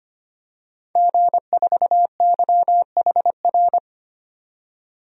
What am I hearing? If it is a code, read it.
Morse code, heard as a keyed tone, about 25 wpm, 700 Hz. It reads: G4YHR
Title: DXCC-100 Call Signs - 5 Characters - 25wpm